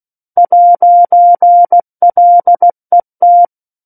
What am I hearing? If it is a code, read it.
Morse code, heard as a keyed tone, about 16 wpm, 700 Hz. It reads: 'LET
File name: 03-Chapter-03 16 WPM